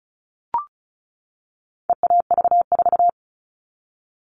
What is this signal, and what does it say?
Morse code, heard as a keyed tone, about 34 wpm, 700 Hz. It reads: EAV4
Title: All Cut Numbers - 4-Digit 35wpm